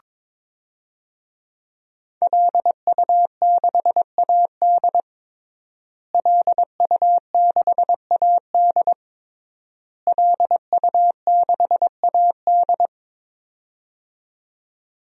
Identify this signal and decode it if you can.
Morse code, heard as a keyed tone, about 22 wpm, 700 Hz. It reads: LU6AD LU6AD LU6AD
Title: DXCC-100 Call Signs - 5 Characters - Repeated 3x - 22wpm